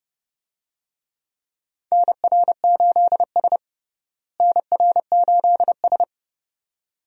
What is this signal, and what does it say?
Morse code, heard as a keyed tone, about 29 wpm, 700 Hz. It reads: NR8S NR8S
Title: USA Call Signs - 4 Characters - Repeated 2x - 30wpm